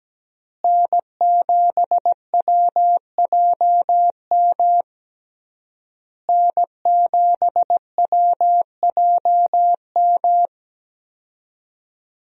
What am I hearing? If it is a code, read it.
Morse code, heard as a keyed tone, about 17 wpm, 700 Hz. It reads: N7WJM N7WJM
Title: USA Call Signs - 5 Characters - Repeated 3x - 17wpm